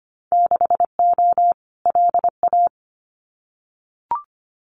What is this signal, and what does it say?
Morse code, heard as a keyed tone, about 25 wpm, 700 Hz. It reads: 6O LA
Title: Winter Field Day Exchanges Encore 25wpm